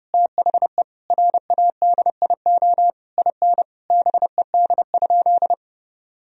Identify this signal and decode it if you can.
Morse code, heard as a keyed tone, about 30 wpm, 700 Hz. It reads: THE RADIO IN BED?
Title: Sentences from Top 1000 Words - Review Part 3 - 30wpm